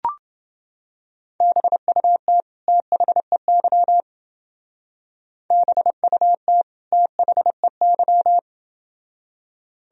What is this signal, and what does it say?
Morse code, heard as a keyed tone, about 29 wpm, 700 Hz. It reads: BUT THEY BUT THEY